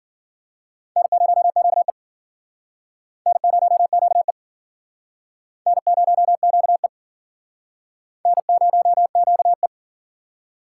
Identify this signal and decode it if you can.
Morse code, heard as a keyed tone, about 53 wpm, 700 Hz. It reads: N0QE N0QE N0QE N0QE